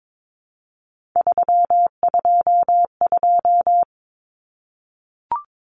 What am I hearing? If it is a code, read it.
Morse code, heard as a keyed tone, about 22 wpm, 700 Hz. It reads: 322
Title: Numbers - 3-Digit 22wpm